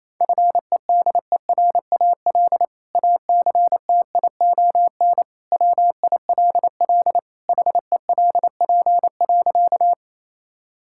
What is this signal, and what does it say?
Morse code, heard as a keyed tone, about 28 wpm, 700 Hz. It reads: FEDERAL ACTION WILL HELP.
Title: Sentences from Top 1000 Words - Review Part 5 - 28wpm